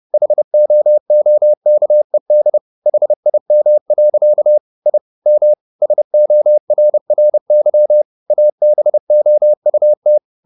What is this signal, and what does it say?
Morse code, heard as a keyed tone, about 30 wpm, 600 Hz. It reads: HOOKED HIM. I M SORRY ABOUT